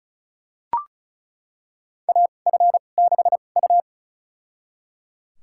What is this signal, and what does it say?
Morse code, heard as a keyed tone, about 36 wpm, spaced as for 25 wpm, 700 Hz. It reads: AF6U
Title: USA Call Signs - 4 Characters - 2x Character Spacing - 35wpm